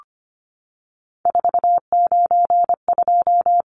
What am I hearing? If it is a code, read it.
Morse code, heard as a keyed tone, about 25 wpm, 700 Hz. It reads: 492
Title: Numbers - 3-Digit 25wpm